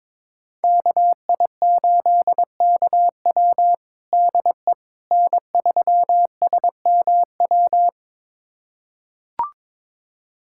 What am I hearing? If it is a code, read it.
Morse code, heard as a keyed tone, about 22 wpm, 700 Hz. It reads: KI8KW DE N3SMW
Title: USA Callsign DE Callsign - 5 Characters - 22wpm